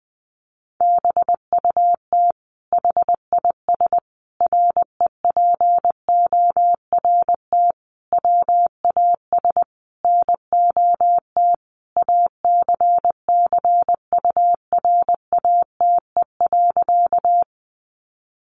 Morse code, heard as a keyed tone, about 20 wpm, 700 Hz. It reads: BUT HIS REPORT WAS NOT ACCURATE.
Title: Sentences from Top 1200 Words 20wpm